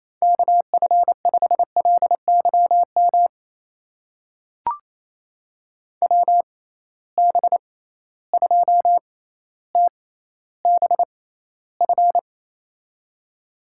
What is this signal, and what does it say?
Morse code, heard as a keyed tone, about 28 wpm, spaced as for 14 wpm, 700 Hz. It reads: KF5LYM WB2TBF
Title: USA Call Signs - 6 Characters - 6x Character Spacing - 28wpm